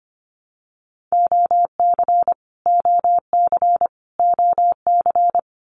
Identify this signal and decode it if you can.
Morse code, heard as a keyed tone, about 25 wpm, 700 Hz. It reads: OC OC OC
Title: Abbreviations Repeated 3x - 25wpm